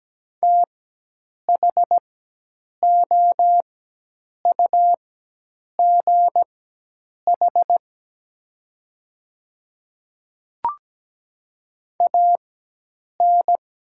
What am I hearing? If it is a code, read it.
Morse code, heard as a keyed tone, about 17 wpm, spaced as for 8 wpm, 700 Hz. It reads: THOUGH AN
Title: 4x Character Spacing - Top 500 Words - 17wpm